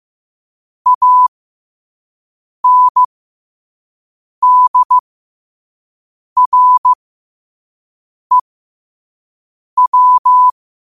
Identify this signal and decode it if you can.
Morse code, heard as a keyed tone, about 15 wpm, spaced as for 5 wpm, 1000 Hz. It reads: ANDREW
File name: qotdfiles / 5wpm